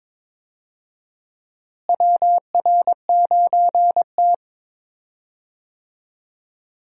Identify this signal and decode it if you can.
Morse code, heard as a keyed tone, about 21 wpm, 700 Hz. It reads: WR9T
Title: USA Call Signs - 4 Characters - Repeated 2x - 22wpm